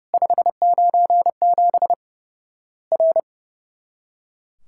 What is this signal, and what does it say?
Morse code, heard as a keyed tone, about 30 wpm, 700 Hz. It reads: 597 R
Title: Worked All Europe DX Contest - QTC Exchange - 30wpm